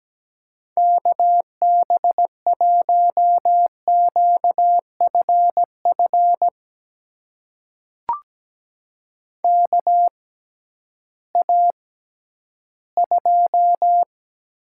Morse code, heard as a keyed tone, about 17 wpm, spaced as for 11 wpm, 700 Hz. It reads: KB1QFF KA2
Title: USA Call Signs - 6 Characters - 6x Character Spacing - 17wpm